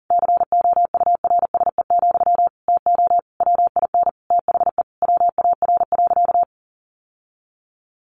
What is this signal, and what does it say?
Morse code, heard as a keyed tone, about 37 wpm, 700 Hz. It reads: COURSE, TO WIN THE WAR.
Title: Sentences from Top 1500 Words 40wpm